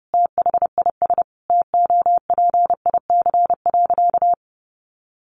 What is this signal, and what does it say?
Morse code, heard as a keyed tone, about 30 wpm, 700 Hz. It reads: THIS TOPIC.
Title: Sentences from Top 1200 Words 30wpm